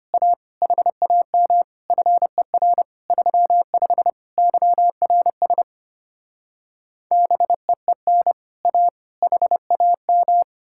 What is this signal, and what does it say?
Morse code, heard as a keyed tone, about 33 wpm, 700 Hz. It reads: A HAM FER 35 YRS BEEN A HAM